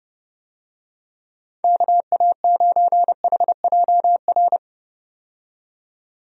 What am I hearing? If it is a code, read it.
Morse code, heard as a keyed tone, about 29 wpm, 700 Hz. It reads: KA9HJR
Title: USA Call Signs - 6 Characters - 30wpm